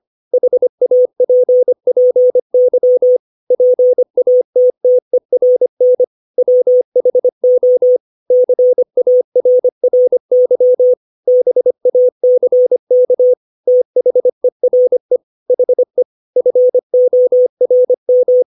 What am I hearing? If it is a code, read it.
Morse code, heard as a keyed tone, about 25 wpm, 500 Hz. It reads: HAPPY PATTERN WHO CARRY BACK THERE HE FORM
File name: 632 words / 632 Words 25WPM